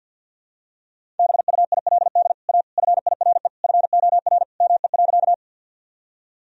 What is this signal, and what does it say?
Morse code, heard as a keyed tone, about 50 wpm, 700 Hz. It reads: BUILD A FIRE FOR ME.
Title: Sentences from Top 1000 Words - Review Part 1 - 50wpm